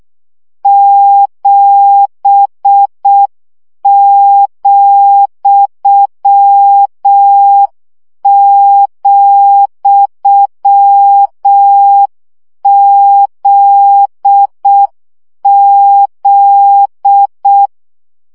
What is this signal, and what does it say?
Morse code, heard as a keyed tone, about 6 wpm, 800 Hz. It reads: 7,,ZZ
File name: CW gyakorlas z 7 vesszo 5 baud